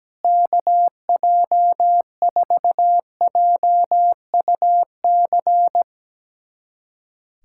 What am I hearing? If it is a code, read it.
Morse code, heard as a keyed tone, about 17 wpm, 700 Hz. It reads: KJ4JUC
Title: USA Callsign DE Callsign 17wpm